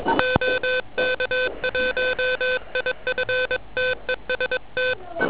Morse code, heard as a keyed tone, about 22 wpm, 3550 Hz, with noise. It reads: OK1IF TEST
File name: OK QRP 2007 / REC28